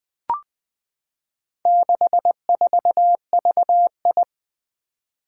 Morse code, heard as a keyed tone, about 21 wpm, 700 Hz. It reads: 64VI
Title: Four Challenging Letter-Number Combinations 20wpm